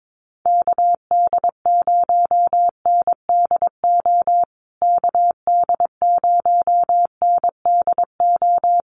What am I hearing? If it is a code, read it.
Morse code, heard as a keyed tone, about 22 wpm, 700 Hz. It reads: KD0NDO KD0NDO
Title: USA Call Signs Repeated 3x 22wpm